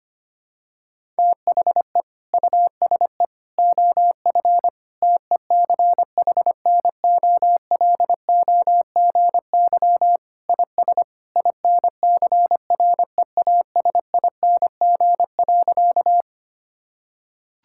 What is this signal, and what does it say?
Morse code, heard as a keyed tone, about 25 wpm, 700 Hz. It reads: THE USE OF TECHNOLOGY IS INCREASING.
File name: practice / 25wpm / Sentences with 2-Word Phrases - 25wpm